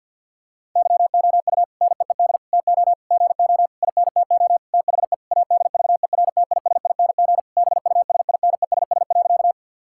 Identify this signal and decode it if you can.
Morse code, heard as a keyed tone, about 52 wpm, 700 Hz. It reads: YOU NEED TO GO INTO THE ADVERTISING BUSINESS.
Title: Sentences from Top 1000 Words - Review Part 6 - 50wpm